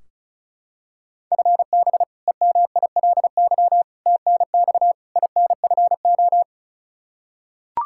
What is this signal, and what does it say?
Morse code, heard as a keyed tone, about 34 wpm, 700 Hz. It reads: FB EMILY TNX INFO
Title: Practice QSO - Exchange 3 - More Info - 35wpm